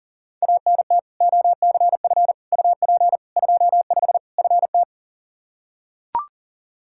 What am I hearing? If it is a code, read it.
Morse code, heard as a keyed tone, about 40 wpm, 700 Hz. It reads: ANT OCF UP 25 FT
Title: Practice QSO - Exchange 2 - Info - 40wpm